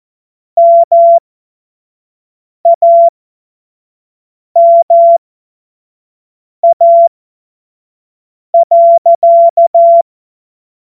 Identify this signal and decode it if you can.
Morse code, heard as a keyed tone, about 14 wpm, spaced as for 5 wpm, 680 Hz. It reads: MAMA.
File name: Morsekurs / Witze / Witz 06. April 2021, 25 F70